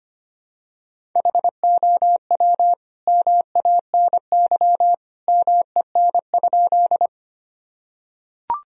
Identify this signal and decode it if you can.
Morse code, heard as a keyed tone, about 25 wpm, 700 Hz. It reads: HOW MANY MEN?
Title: Sentences from Top 1000 Words - Review Part 1 - 25wpm